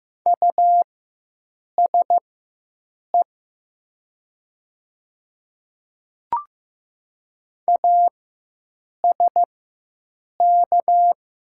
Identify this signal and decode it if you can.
Morse code, heard as a keyed tone, about 15 wpm, spaced as for 7 wpm, 700 Hz. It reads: USE ASK
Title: 4x Character Spacing - Top 200 Words - 15wpm